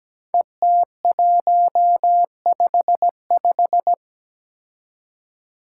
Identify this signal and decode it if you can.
Morse code, heard as a keyed tone, about 17 wpm, 700 Hz. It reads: ET155
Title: Five Challenging Letter-Number Combinations 17wpm